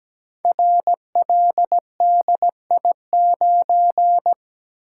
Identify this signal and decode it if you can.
Morse code, heard as a keyed tone, about 17 wpm, 700 Hz. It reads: RLDI9
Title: Code Groups - 5 Letter:Numbers - 17wpm